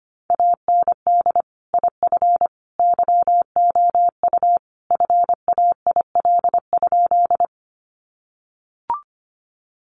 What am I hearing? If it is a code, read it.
Morse code, heard as a keyed tone, about 25 wpm, 700 Hz. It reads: AND IF YOU FAIL?
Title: Sentences from Top 1100 Words 25wpm